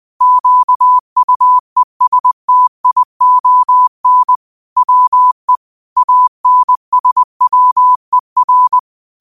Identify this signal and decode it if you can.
Morse code, heard as a keyed tone, about 20 wpm, 1000 Hz. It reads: QUESTION WE ANSWER
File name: 20wpm